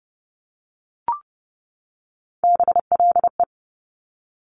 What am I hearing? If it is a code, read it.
Morse code, heard as a keyed tone, about 30 wpm, 690 Hz. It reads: BLE
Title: Top Three Letter Combinations in Words 30wpm